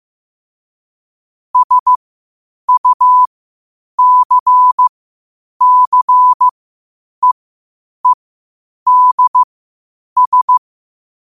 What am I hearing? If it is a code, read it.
Morse code, heard as a keyed tone, about 15 wpm, spaced as for 8 wpm, 1000 Hz. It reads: SUCCEEDS
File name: qotdfiles / 8wpm